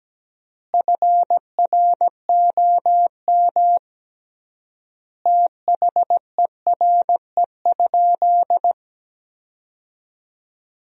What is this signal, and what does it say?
Morse code, heard as a keyed tone, about 17 wpm, 700 Hz. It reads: FROM THERE?
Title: 3x Word-Spacing - Sentences from Top 100 Words - 17wpm